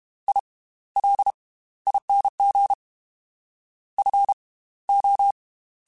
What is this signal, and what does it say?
Morse code, heard as a keyed tone, about 32 wpm, spaced as for 17 wpm, 800 Hz. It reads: ILING FO